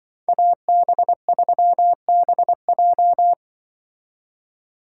Morse code, heard as a keyed tone, about 24 wpm, 700 Hz. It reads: AB3BJ
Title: Speed Racing - USA 2x2 Call Signs 20wpm